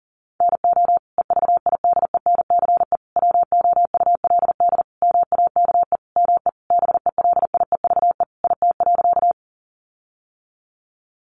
Morse code, heard as a keyed tone, about 40 wpm, 700 Hz. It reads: NO EVIDENCE WOULD MAKE ME BELIEVE IT.
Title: Sentences from Top 1300 Words 40wpm